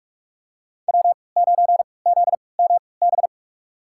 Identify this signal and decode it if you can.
Morse code, heard as a keyed tone, about 43 wpm, spaced as for 25 wpm, 700 Hz. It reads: W9ZMB
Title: USA Call Signs - 5 Characters - 3x Character Spacing - 45wpm